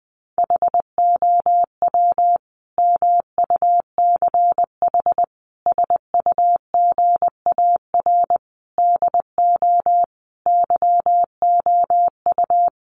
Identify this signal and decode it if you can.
Morse code, heard as a keyed tone, about 20 wpm, 700 Hz. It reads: HOW MUCH SUGAR DO YOU
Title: Sentences from Top 1200 Words 20wpm